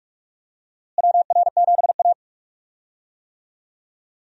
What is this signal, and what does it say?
Morse code, heard as a keyed tone, about 43 wpm, 700 Hz. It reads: WR7A
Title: USA Call Signs - 4 Characters - Repeated 2x - 45wpm